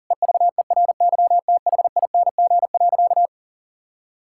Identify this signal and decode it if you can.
Morse code, heard as a keyed tone, about 40 wpm, 700 Hz. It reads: EVERYTHING.
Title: Sentences from Top 1600 Words 40wpm